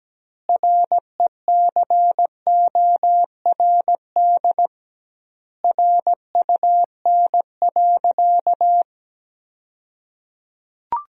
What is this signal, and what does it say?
Morse code, heard as a keyed tone, about 17 wpm, 700 Hz. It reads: RECORD RUN.
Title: 2x Word-Spacing - Sentences from Top 500 Words - 17wpm